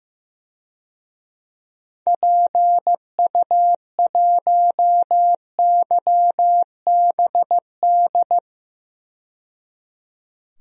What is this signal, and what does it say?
Morse code, heard as a keyed tone, about 15 wpm, 700 Hz. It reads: PU1YBD
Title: DXCC-100 Call Signs - 6 Characters - 15wpm